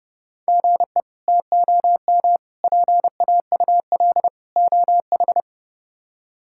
Morse code, heard as a keyed tone, about 30 wpm, 700 Hz. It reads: GE TOM PAUL OH
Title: K1USN SST QSO Practice - 30wpm